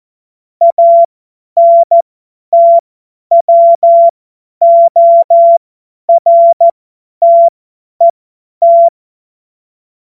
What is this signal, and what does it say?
Morse code, heard as a keyed tone, about 14 wpm, spaced as for 10 wpm, 680 Hz. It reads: ANTWORTET
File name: Morsekurs / Witze / Witz 30. November 2021, 50 F70